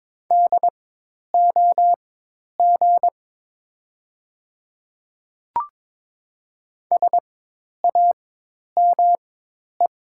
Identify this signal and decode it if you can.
Morse code, heard as a keyed tone, about 22 wpm, spaced as for 10 wpm, 700 Hz. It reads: DOG SAME
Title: 4x Character Spacing - Top 500 Words - 22wpm